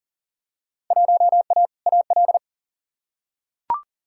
Morse code, heard as a keyed tone, about 40 wpm, 700 Hz. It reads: 1A AL
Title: Field Day Exchanges - 40wpm